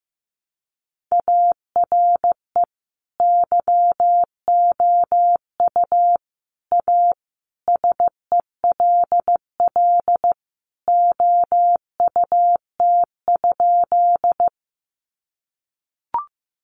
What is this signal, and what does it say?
Morse code, heard as a keyed tone, about 15 wpm, 700 Hz. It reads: ARE YOU A SELL OUT?
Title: Sentences from Top 1100 Words 15wpm